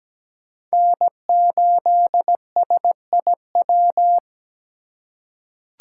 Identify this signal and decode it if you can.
Morse code, heard as a keyed tone, about 17 wpm, 700 Hz. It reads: N8SIW
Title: USA Call Signs - 5 Characters - 8x Character Spacing - 17wpm